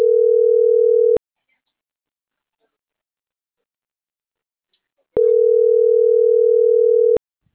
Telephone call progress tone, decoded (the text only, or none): ringback tone